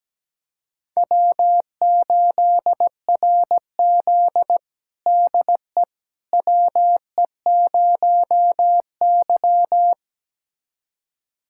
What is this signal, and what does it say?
Morse code, heard as a keyed tone, about 17 wpm, 700 Hz. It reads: W8RZ DE WE0Y
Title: USA Callsign DE Callsign - 4 Characters - 17wpm